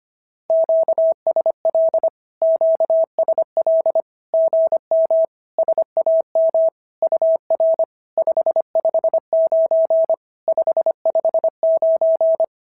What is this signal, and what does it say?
Morse code, heard as a keyed tone, about 25 wpm, 650 Hz. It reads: QSL QSL GM SAM UR 559 559
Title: SOTA QSO Practice 25wpm